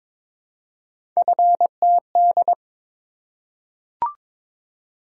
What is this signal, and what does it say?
Morse code, heard as a keyed tone, about 22 wpm, 700 Hz. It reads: FTD
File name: practice / 22wpm / Three Letter-Number Combinations - 22wpm